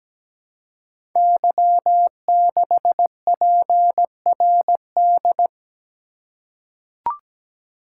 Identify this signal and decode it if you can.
Morse code, heard as a keyed tone, about 17 wpm, 700 Hz. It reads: Y6PRD
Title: Code Groups - 5 Letter:Numbers - 17wpm